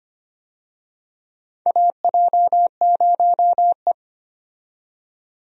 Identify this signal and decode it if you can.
Morse code, heard as a keyed tone, about 25 wpm, 700 Hz. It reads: AJ0E
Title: USA Call Signs - 4 Characters - 25wpm